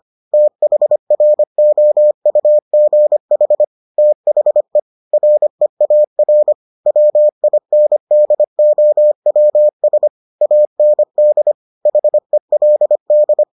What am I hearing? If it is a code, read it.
Morse code, heard as a keyed tone, about 25 wpm, 600 Hz. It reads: THROUGH THE REAR WINDOWS AND HELD